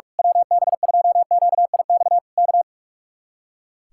Morse code, heard as a keyed tone, about 46 wpm, 700 Hz. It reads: WB2QIX K
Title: Speed-Racing Common QSO Phrases - 45wpm